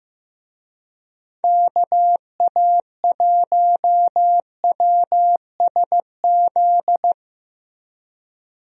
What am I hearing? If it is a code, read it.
Morse code, heard as a keyed tone, about 15 wpm, 700 Hz. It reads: KA1WSZ
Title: USA Call Signs - 6 Characters - Repeated 2x - 15wpm